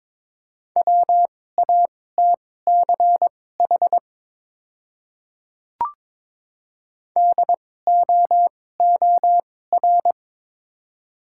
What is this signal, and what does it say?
Morse code, heard as a keyed tone, about 22 wpm, spaced as for 16 wpm, 700 Hz. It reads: WATCH DOOR
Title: 2x Character Spacing - Top 300 Words - 22wpm